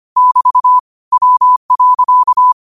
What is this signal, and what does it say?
Morse code, heard as a keyed tone, about 25 wpm, 1000 Hz. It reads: <BT> W.